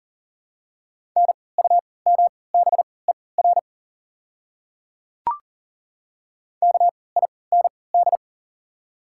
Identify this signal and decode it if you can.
Morse code, heard as a keyed tone, about 40 wpm, spaced as for 23 wpm, 700 Hz. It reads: NUMBER KIND
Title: 3x Character Spacing - Top 200 Words - 40wpm